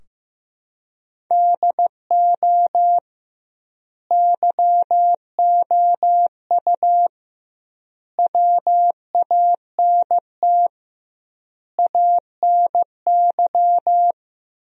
Morse code, heard as a keyed tone, about 15 wpm, 700 Hz. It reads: DO YOU WANT ANY
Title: 2x Word-Spacing - Sentences from Top 400 Words - 15wpm